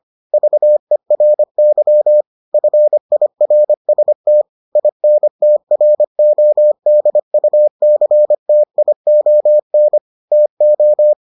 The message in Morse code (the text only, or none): VERY FIRST INTRODUCTION TO